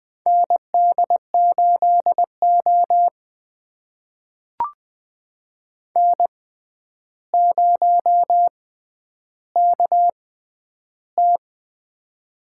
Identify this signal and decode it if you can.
Morse code, heard as a keyed tone, about 20 wpm, spaced as for 10 wpm, 700 Hz. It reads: ND8O N0KT
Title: USA Call Signs - 4 Characters - 6x Character Spacing - 20wpm